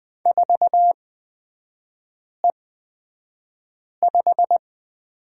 Morse code, heard as a keyed tone, about 20 wpm, spaced as for 5 wpm, 700 Hz. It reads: 4E5
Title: 236 Lesson 38e - Letter q - Three Characters 20wpm